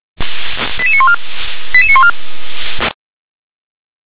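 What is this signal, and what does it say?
Five-tone selective call (ZVEI), heard as quick runs of five tones, tones of about 80 ms. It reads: 88914 88914
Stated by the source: ZVEI five-tone call received by SDR.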